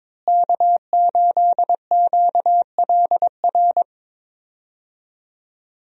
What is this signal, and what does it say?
Morse code, heard as a keyed tone, about 22 wpm, 700 Hz. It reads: K8QLR
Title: USA Call Signs - 5 Characters - Repeated 3x - 22wpm